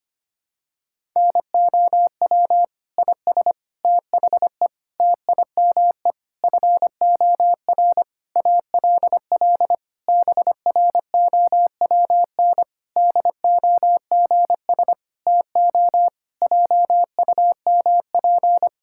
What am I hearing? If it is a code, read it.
Morse code, heard as a keyed tone, about 25 wpm, 700 Hz. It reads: NOW IS THE TIME FOR ALL BROWN DOGS TO JUMP
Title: Pangram Challenge v2 - 25wpm